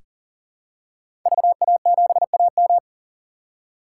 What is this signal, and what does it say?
Morse code, heard as a keyed tone, about 40 wpm, 700 Hz. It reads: VA7AM
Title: K1USN SST QSO Practice - 40wpm